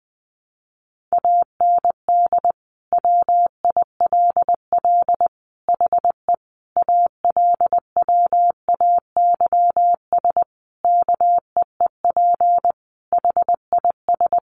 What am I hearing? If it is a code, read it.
Morse code, heard as a keyed tone, about 20 wpm, 700 Hz. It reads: AND WILL HE ALWAYS KEEP HIS
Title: Sentences from Top 1200 Words 20wpm